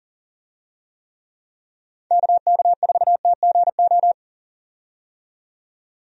Morse code, heard as a keyed tone, about 40 wpm, 700 Hz. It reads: KK4TGO
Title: USA Call Signs - 6 Characters - 40wpm